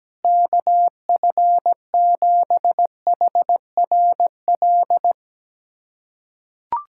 Morse code, heard as a keyed tone, about 17 wpm, 700 Hz. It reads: KF7HRL
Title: USA Callsign DE Callsign - 6 Characters - 17wpm